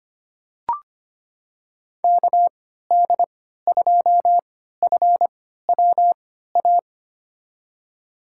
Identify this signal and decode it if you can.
Morse code, heard as a keyed tone, about 25 wpm, spaced as for 14 wpm, 690 Hz. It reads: KD2FWA
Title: USA Call Signs - 6 Characters - 3x Character Spacing - 25wpm